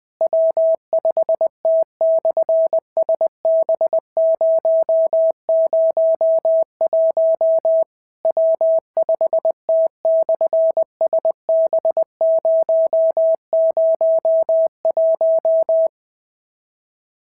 Morse code, heard as a keyed tone, about 20 wpm, 650 Hz. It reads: W5T/SB001 W5T/SB001
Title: SOTA QSO Practice 20wpm